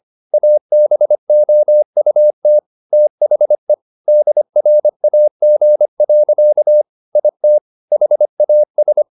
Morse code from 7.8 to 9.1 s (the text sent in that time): HAS